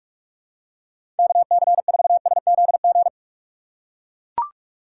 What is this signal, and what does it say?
Morse code, heard as a keyed tone, about 41 wpm, 700 Hz. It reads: KC4SZG
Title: USA Call Signs - 6 Characters - 2x Character Spacing - 45wpm